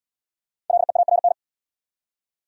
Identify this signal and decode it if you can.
Morse code, heard as a keyed tone, about 68 wpm, 700 Hz. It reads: LEARN